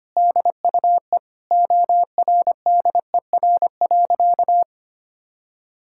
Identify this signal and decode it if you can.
Morse code, heard as a keyed tone, about 25 wpm, 700 Hz. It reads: DUE ORDER.
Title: Sentences from Top 1000 Words - Review Part 6 - 25wpm